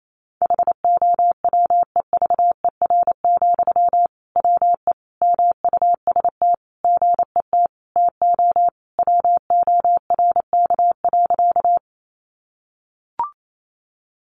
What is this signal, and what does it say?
Morse code, heard as a keyed tone, about 28 wpm, 700 Hz. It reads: HOWEVER, WE MUST GET TO WORK.
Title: Sentences from Top 600 Words 28wpm